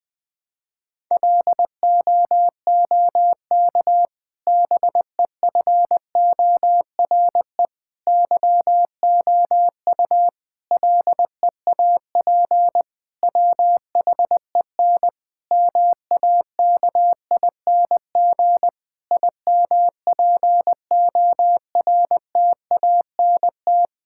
Morse code, heard as a keyed tone, about 21 wpm, 700 Hz. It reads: LOOK BEFORE YOU LEAP WHEN MAKING IMPORTANT
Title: Difficult Sentences with Hints - 20wpm